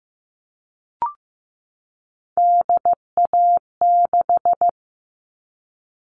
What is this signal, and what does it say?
Morse code, heard as a keyed tone, about 15 wpm, 700 Hz. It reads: DA6
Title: All Cut Numbers - 3-Digit 15wpm